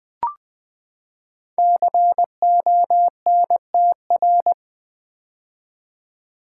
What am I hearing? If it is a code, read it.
Morse code, heard as a keyed tone, about 20 wpm, 700 Hz. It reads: CONTR
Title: Build A Word - 5 Letters At A Time - 12-Letter Words - 20wpm